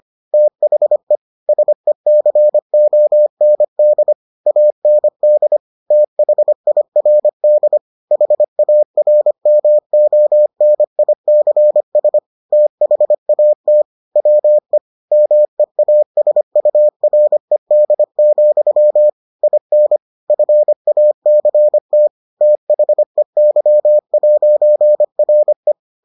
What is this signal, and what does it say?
Morse code, heard as a keyed tone, about 25 wpm, 600 Hz. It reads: THE SECOND AND THIRD HARMONICS THAT WE MEASURED, IN FACT THEY'RE